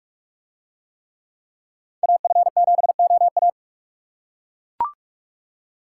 Morse code, heard as a keyed tone, about 41 wpm, 700 Hz. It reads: AF7OA